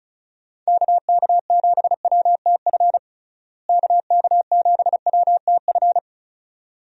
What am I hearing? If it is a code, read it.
Morse code, heard as a keyed tone, about 34 wpm, 700 Hz. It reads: KK7WTF KK7WTF
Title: USA Call Signs - 6 Characters - Repeated 2x - 35wpm